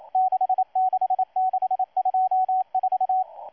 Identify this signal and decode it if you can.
Morse code, heard as a keyed tone, about 28 wpm, 750 Hz. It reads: 66624
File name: M12-2016-07-07-1310utc-13926khz-error-restart